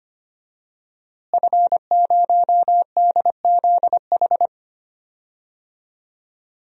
Morse code, heard as a keyed tone, about 25 wpm, 700 Hz. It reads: F0DZH
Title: DXCC-100 Call Signs - 5 Characters - Repeated 2x - 25wpm